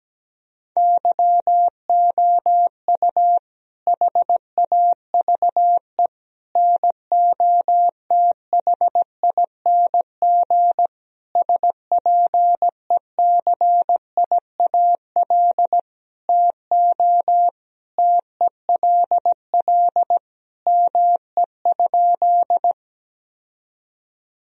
Morse code, heard as a keyed tone, about 17 wpm, 700 Hz. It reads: YOU HAVE NOTHING SPECIAL TO TELL ME?
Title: Sentences from Top 1000 Words - Review Part 3 - 17wpm